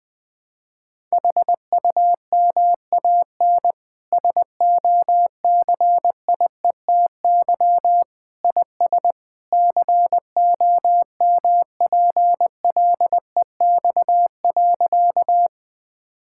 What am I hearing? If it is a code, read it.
Morse code, heard as a keyed tone, about 20 wpm, 700 Hz. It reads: HUMAN SOCIETY IS COMPLEX.
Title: Sentences from Top 1000 Words - Review Part 6 - 20wpm